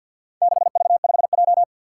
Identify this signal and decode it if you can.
Morse code, heard as a keyed tone, about 46 wpm, 700 Hz. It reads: 6V5J
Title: Four Challenging Letter-Number Combinations 50wpm